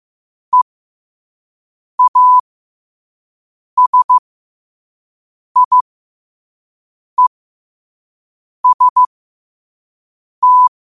Morse code, heard as a keyed tone, about 15 wpm, spaced as for 5 wpm, 1000 Hz. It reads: EASIEST
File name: qotdfiles / 5wpm